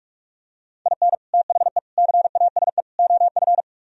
Morse code, heard as a keyed tone, about 49 wpm, 690 Hz. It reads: IN THE CASE OF